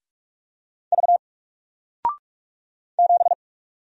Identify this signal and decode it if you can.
Morse code, heard as a keyed tone, about 45 wpm, 700 Hz. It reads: V 7
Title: Single Letter-Number Alt 45wpm